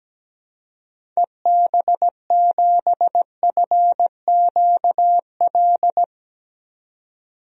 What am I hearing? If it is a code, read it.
Morse code, heard as a keyed tone, about 17 wpm, 700 Hz. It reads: EB7FQL
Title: DXCC-100 Call Signs - 6 Characters - 17wpm